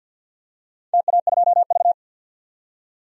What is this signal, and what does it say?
Morse code, heard as a keyed tone, about 47 wpm, 700 Hz. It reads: TA2V
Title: DXCC-100 Call Signs - 4 Characters - 50wpm